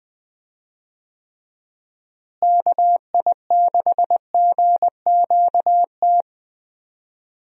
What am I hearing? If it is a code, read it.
Morse code, heard as a keyed tone, about 20 wpm, 700 Hz. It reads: KI6GQT
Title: USA Call Signs - 6 Characters - Repeated 2x - 20wpm